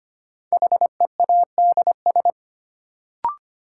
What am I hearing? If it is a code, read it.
Morse code, heard as a keyed tone, about 25 wpm, 700 Hz. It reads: HEADS
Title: Sentences from Top 1000 Words - Review Part 3 - New Word Drill - 25wpm